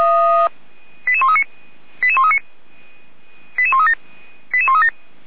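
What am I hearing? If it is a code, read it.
Five-tone selective call (ZVEI), heard as quick runs of five tones, tones of about 70 ms. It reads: 88139 88139 88137 88137
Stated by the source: ZVEI five-tone call received on SDR, Feuerwehr Orlamünde.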